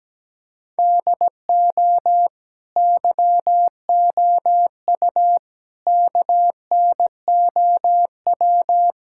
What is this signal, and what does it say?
Morse code, heard as a keyed tone, about 17 wpm, 700 Hz. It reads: DO YOU KNOW
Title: Sentences from Top 1600 Words 17wpm